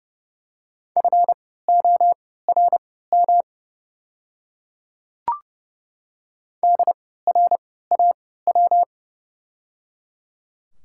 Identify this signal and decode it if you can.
Morse code, heard as a keyed tone, about 30 wpm, spaced as for 17 wpm, 700 Hz. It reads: FORM DRAW
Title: 3x Character Spacing - Top 500 Words - 30wpm